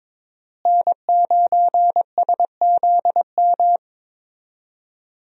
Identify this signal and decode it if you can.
Morse code, heard as a keyed tone, about 22 wpm, 700 Hz. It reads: N9SZM
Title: USA Call Signs - 5 Characters - Repeated 3x - 22wpm